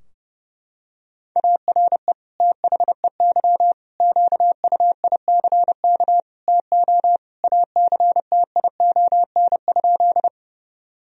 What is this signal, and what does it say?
Morse code, heard as a keyed tone, about 31 wpm, 700 Hz. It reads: ARE THEY QUICK TO ACTION?
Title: Sentences from Top 1000 Words - Review Part 4 - 30wpm